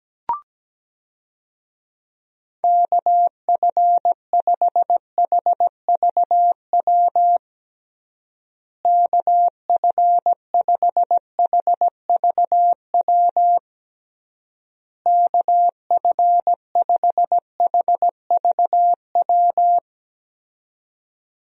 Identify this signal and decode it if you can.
Morse code, heard as a keyed tone, about 17 wpm, 700 Hz. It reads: KF5HVW KF5HVW KF5HVW
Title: USA Call Signs - 6 Characters - Repeated 3x - 17wpm